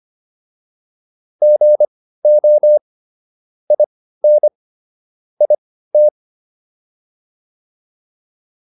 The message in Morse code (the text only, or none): GO IN IT